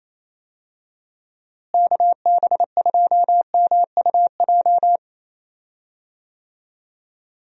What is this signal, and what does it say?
Morse code, heard as a keyed tone, about 28 wpm, 700 Hz. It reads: KB2MUJ
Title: USA Call Signs - 6 Characters - Repeated 3x - 28wpm